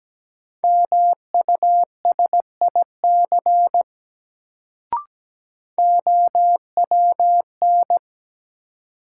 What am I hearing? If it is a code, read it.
Morse code, heard as a keyed tone, about 17 wpm, 700 Hz. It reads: MUSIC OWN
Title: Sentences from Top 300 Words - New Word Drill - 17wpm